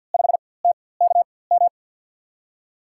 Morse code, heard as a keyed tone, about 53 wpm, spaced as for 24 wpm, 700 Hz. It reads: 5TKM
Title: USA Call Signs - 6 Characters - 4x Character Spacing - 50wpm